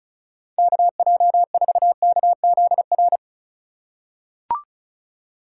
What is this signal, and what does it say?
Morse code, heard as a keyed tone, about 36 wpm, 700 Hz. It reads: KJ4KZR
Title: USA Call Signs - 6 Characters - 2x Character Spacing - 35wpm